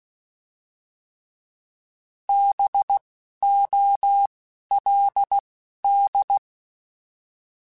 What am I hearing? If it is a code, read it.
Morse code, heard as a keyed tone, about 16 wpm, 800 Hz. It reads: B O L D